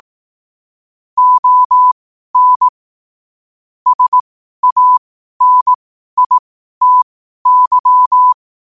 Morse code, heard as a keyed tone, about 18 wpm, spaced as for 12 wpm, 1000 Hz. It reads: ON SANITY
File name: qotdfiles / 12wpm